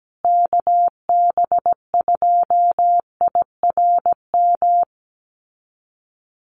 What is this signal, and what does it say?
Morse code, heard as a keyed tone, about 17 wpm, 700 Hz. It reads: KB2IRM
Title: USA Call Signs Repeated 2x 17wpm